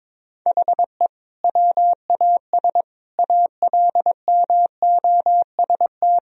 Morse code, heard as a keyed tone, about 22 wpm, 700 Hz. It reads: HE WAS ALMOST
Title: Sentences from Top 1000 Words - Review Part 4 - 22wpm